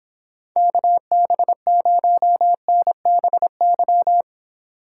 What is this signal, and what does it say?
Morse code, heard as a keyed tone, about 26 wpm, 700 Hz. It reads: KB0NBY